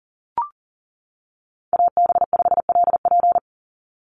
Morse code, heard as a keyed tone, about 40 wpm, 700 Hz. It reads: AB5LP
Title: USA Call Signs 40wpm